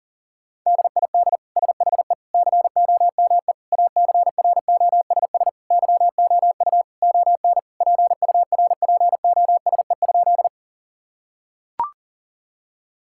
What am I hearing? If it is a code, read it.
Morse code, heard as a keyed tone, about 37 wpm, 700 Hz. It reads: DID SHE COME ACROSS YOU ON PURPOSE?
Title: Sentences from Top 1000 Words - Review Part 5 - 40wpm